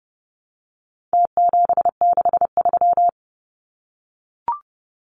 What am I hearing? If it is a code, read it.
Morse code, heard as a keyed tone, about 30 wpm, 700 Hz. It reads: T763
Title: Numbers - Contest Style - 4-Digit 30wpm